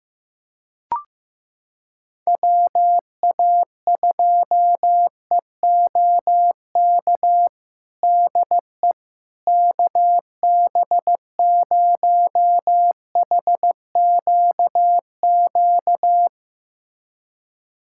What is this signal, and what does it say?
Morse code, heard as a keyed tone, about 15 wpm, 700 Hz. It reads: WA2EOK DE KB0HQQ
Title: USA Callsign DE Callsign - 6 Characters - 15wpm